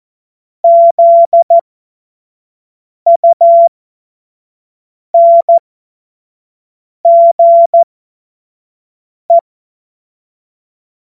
Morse code, heard as a keyed tone, about 14 wpm, spaced as for 5 wpm, 680 Hz. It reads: ZUNGE
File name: Morsekurs / Witze / Witz 19 März 2024 25 F70